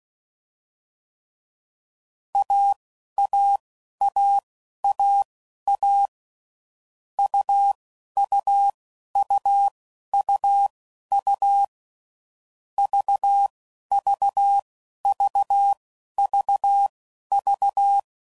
Morse code, heard as a keyed tone, about 16 wpm, spaced as for 11 wpm, 800 Hz. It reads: AAAAA UUUUU VVVVV
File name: Combinations / Difficult Combos / Difficult Combos